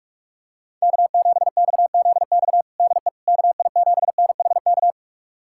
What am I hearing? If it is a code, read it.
Morse code, heard as a keyed tone, about 46 wpm, 700 Hz. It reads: K7XZX DE KI7NHK
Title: USA Callsign DE Callsign 45wpm